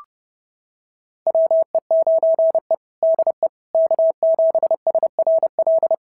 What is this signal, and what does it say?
Morse code, heard as a keyed tone, about 30 wpm, 650 Hz. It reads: WE9E DE K7SRL
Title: SOTA QSO Practice 30wpm